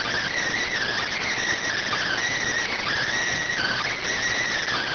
Emergency Alert System (SAME) tones, heard as harsh buzzing bursts. It is heard from 0.6 to 4.0 s.